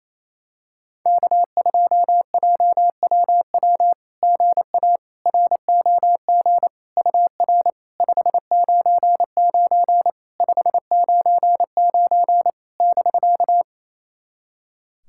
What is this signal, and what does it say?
Morse code, heard as a keyed tone, about 28 wpm, 690 Hz. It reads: K2JWW GA ROG UR 599 599 <BK>
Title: SOTA QSO Practice 28wpm